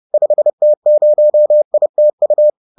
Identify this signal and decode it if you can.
Morse code, heard as a keyed tone, about 30 wpm, 600 Hz. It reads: 5T0ITU
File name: 30wpm-7